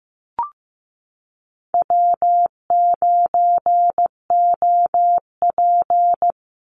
Morse code, heard as a keyed tone, about 15 wpm, 700 Hz. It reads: W9OP
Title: Call Signs Repeated 2x - 15wpm